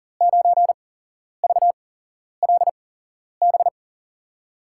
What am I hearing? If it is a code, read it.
Morse code, heard as a keyed tone, about 37 wpm, spaced as for 11 wpm, 700 Hz. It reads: 9VLB
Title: USA Call Signs - 6 Characters - 8x Character Spacing - 40wpm